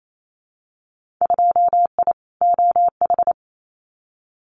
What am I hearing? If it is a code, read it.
Morse code, heard as a keyed tone, about 28 wpm, 700 Hz. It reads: 2I OH
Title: Winter Field Day Exchanges 28wpm